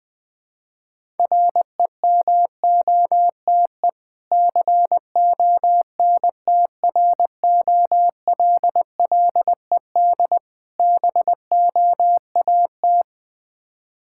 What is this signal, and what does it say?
Morse code, heard as a keyed tone, about 20 wpm, 700 Hz. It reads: REMOTE CONTROLLED BOAT
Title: Difficult Words with Hints - 20wpm